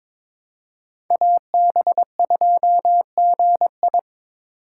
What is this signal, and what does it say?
Morse code, heard as a keyed tone, about 22 wpm, 700 Hz. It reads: AB2GI
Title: K1USN SST QSO Practice - 22wpm